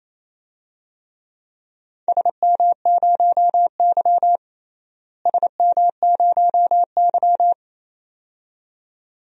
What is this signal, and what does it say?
Morse code, heard as a keyed tone, about 28 wpm, 700 Hz. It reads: SM0Y SM0Y
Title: DXCC-100 Call Signs - 4 Characters - Repeated 2x - 28wpm